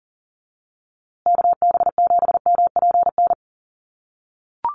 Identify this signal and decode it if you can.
Morse code, heard as a keyed tone, about 39 wpm, 700 Hz. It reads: KB7MPN
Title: USA Call Signs Repeated 3x 40wpm